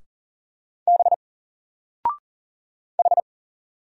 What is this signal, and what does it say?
Morse code, heard as a keyed tone, about 40 wpm, 700 Hz. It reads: B H
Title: Single Letter-Number 40wpm